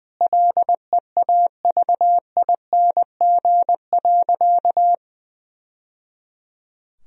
Morse code, heard as a keyed tone, about 20 wpm, 700 Hz. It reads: LEAVING.
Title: Sentences from Top 1000 Words - Review Part 4 - 20wpm